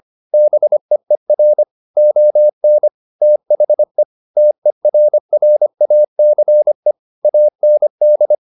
Morse code, heard as a keyed tone, about 25 wpm, 600 Hz. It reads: BEER ON THE TERRACE AND
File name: the old man and the sea part 1 25wpm